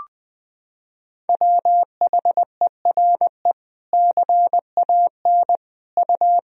Morse code, heard as a keyed tone, about 20 wpm, 700 Hz. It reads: WHERE CAN U
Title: Sentences from Top 500 Words and Abbreviations 20wpm